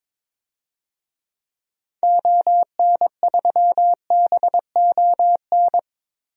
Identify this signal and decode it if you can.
Morse code, heard as a keyed tone, about 22 wpm, 700 Hz. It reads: ON3BON
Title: DXCC-100 Call Signs - 6 Characters - Repeated 2x - 22wpm